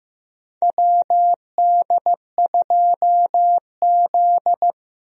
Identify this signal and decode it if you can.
Morse code, heard as a keyed tone, about 15 wpm, 700 Hz. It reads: WD2Z
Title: USA Call Signs - 4 Characters - 15wpm